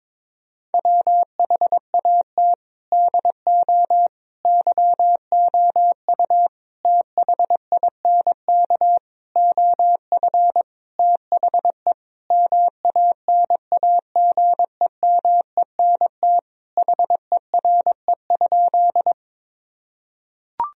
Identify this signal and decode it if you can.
Morse code, heard as a keyed tone, about 22 wpm, 700 Hz. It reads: WHAT DO YOU THINK OF THE MANAGEMENT HERE?
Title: Sentences from Top 1000 Words - Review Part 4 - 22wpm